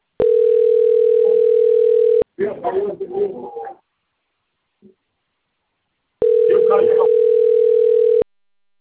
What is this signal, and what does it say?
Ringback tone (North America), 440 + 480 Hz, with speech.
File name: out- - - - - .45454